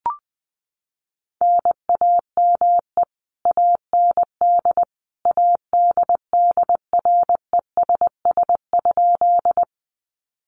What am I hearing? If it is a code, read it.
Morse code, heard as a keyed tone, about 20 wpm, 700 Hz. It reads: NAME AND ADDRESS?
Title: Sentences from Top 1000 Words 20wpm